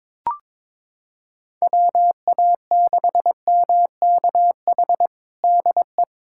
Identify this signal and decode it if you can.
Morse code, heard as a keyed tone, about 22 wpm, 700 Hz. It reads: WA6MKH DE
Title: K1USN SST QSO Practice 22wpm